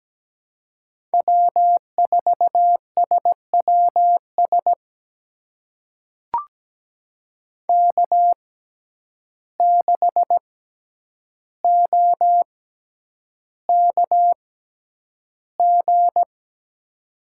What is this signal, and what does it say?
Morse code, heard as a keyed tone, about 17 wpm, spaced as for 9 wpm, 700 Hz. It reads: W4SWS K6OKG
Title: USA Call Signs - 5 Characters - 6x Character Spacing - 17wpm